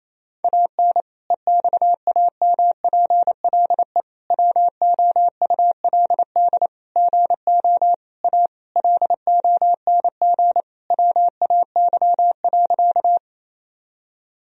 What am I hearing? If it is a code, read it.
Morse code, heard as a keyed tone, about 28 wpm, 700 Hz. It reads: AN EXAMPLE WOULD GO A LONG WAY.
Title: Sentences from Top 1000 Words - Review Part 1 - 28wpm